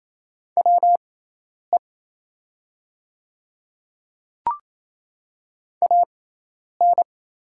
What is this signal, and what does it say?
Morse code, heard as a keyed tone, about 27 wpm, spaced as for 10 wpm, 690 Hz. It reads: WE AN
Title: 6x Character Spacing - Top 100 Words - 28wpm